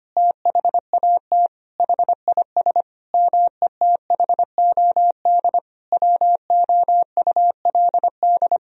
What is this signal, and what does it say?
Morse code, heard as a keyed tone, about 25 wpm, 700 Hz. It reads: THAT HIS METHOD WOULD
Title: Sentences from Top 1000 Words - Review Part 3 - 25wpm